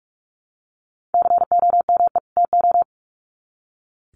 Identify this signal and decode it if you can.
Morse code, heard as a keyed tone, about 44 wpm, 700 Hz. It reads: COME TO
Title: Sets of 2 Words 45wpm